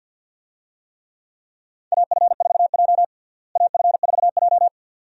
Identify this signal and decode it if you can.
Morse code, heard as a keyed tone, about 52 wpm, 700 Hz. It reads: AF4J AF4J
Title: USA Call Signs - 4 Characters - Repeated 2x - 50wpm